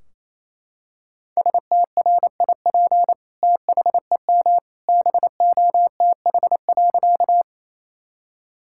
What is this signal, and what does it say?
Morse code, heard as a keyed tone, about 28 wpm, 700 Hz. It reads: STRIP THEM BOTH.
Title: Sentences from Top 1600 Words 28wpm